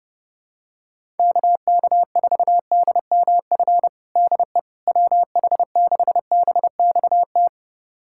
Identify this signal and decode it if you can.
Morse code, heard as a keyed tone, about 30 wpm, 700 Hz. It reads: KK4DMF DE WH6BXT
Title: USA Callsign DE Callsign 30wpm